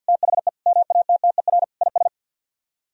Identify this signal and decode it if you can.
Morse code, heard as a keyed tone, about 64 wpm, 700 Hz. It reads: THE MATTER IS